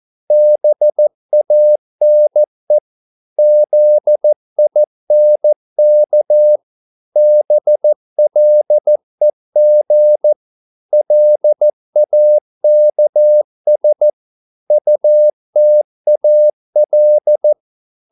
Morse code, heard as a keyed tone, about 14 wpm, 600 Hz. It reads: BANE ZINK BLEG LAKS UTAL
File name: Korte ord DK 14wpm